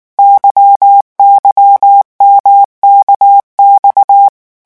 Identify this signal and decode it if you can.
Morse code, heard as a keyed tone, about 19 wpm, 800 Hz. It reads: YYMKX
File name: revision1-vitesse 18 mots